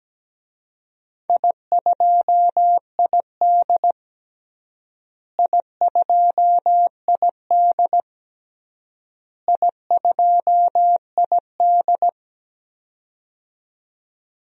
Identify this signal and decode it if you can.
Morse code, heard as a keyed tone, about 17 wpm, 700 Hz. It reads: I2ID I2ID I2ID
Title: DXCC-100 Call Signs - 4 Characters - Repeated 3x - 17wpm